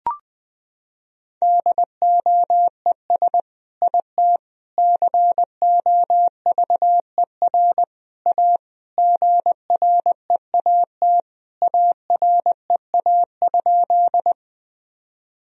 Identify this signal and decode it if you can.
Morse code, heard as a keyed tone, about 20 wpm, 700 Hz. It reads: DOES IT COVER A GREAT AREA?
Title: Sentences from Top 1000 Words - Review Part 2 - 20wpm